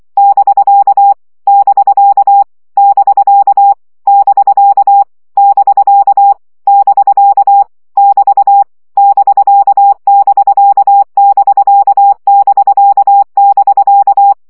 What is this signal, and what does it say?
Morse code, heard as a keyed tone, about 24 wpm, 800 Hz. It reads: <BK> <BK> <BK> <BK> <BK> <BK> <BT> <BK><BK><BK><BK><BK>
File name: CW gyakorlas bk 20 baud